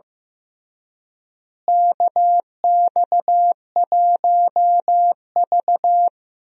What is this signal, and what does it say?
Morse code, heard as a keyed tone, about 15 wpm, 700 Hz. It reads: KX1V
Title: USA Call Signs - 4 Characters - Repeated 3x - 15wpm